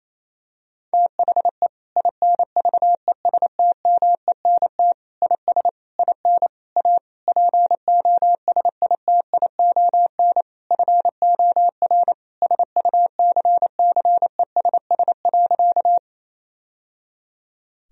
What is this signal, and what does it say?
Morse code, heard as a keyed tone, about 28 wpm, 700 Hz. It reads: THE INVESTMENT IS IN A POSITION FOR SUCCESS.
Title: Sentences from Top 1000 Words - Review Part 4 - 28wpm